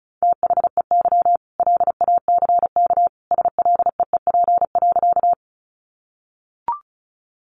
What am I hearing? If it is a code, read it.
Morse code, heard as a keyed tone, about 34 wpm, 700 Hz. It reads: THEY LACK SLEEP.
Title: Sentences from Top 1100 Words 35wpm